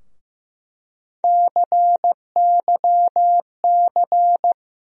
Morse code, heard as a keyed tone, about 15 wpm, 700 Hz. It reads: CYC
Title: USA Call Sign Postfixes - 3 Letters 15wpm